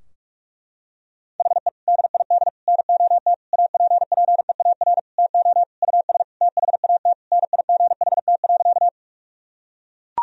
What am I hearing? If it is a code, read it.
Morse code, heard as a keyed tone, about 41 wpm, 700 Hz. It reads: HE DID NOT APPEAR TO US THAT NIGHT.
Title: Sentences from Top 1000 Words - Review Part 2 - 45wpm